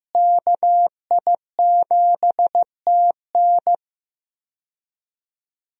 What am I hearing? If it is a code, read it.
Morse code, heard as a keyed tone, about 15 wpm, 700 Hz. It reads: KI7TN
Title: USA Call Signs - 5 Characters - Repeated 2x - 15wpm